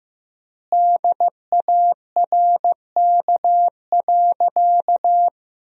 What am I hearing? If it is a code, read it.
Morse code, heard as a keyed tone, about 15 wpm, 700 Hz. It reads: DARK.
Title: 3x Word-Spacing - Sentences from Top 500 Words - 15wpm